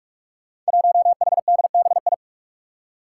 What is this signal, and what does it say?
Morse code, heard as a keyed tone, about 44 wpm, 700 Hz. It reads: 1HDBI
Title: Five Challenging Letter-Number Combinations 45wpm